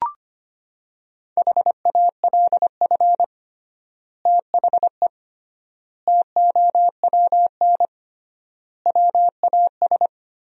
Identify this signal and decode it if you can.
Morse code, heard as a keyed tone, about 25 wpm, 700 Hz. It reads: HALF THE TOWN WAS
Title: 3x Word-Spacing - Sentences from Top 500 Words - 25wpm